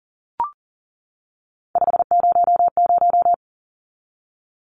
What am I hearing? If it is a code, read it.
Morse code, heard as a keyed tone, about 40 wpm, 700 Hz. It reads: 500
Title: Numbers - 3-Digit 40wpm